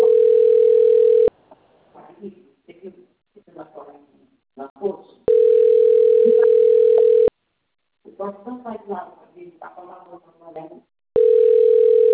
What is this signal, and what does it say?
Ringback tone (North America), 440 + 480 Hz, with speech.
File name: rg-1600-unknown- - - .23073